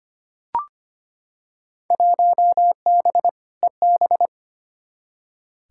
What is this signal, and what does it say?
Morse code, heard as a keyed tone, about 25 wpm, 700 Hz. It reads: 1B EB
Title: Field Day Exchanges Encore - 25wpm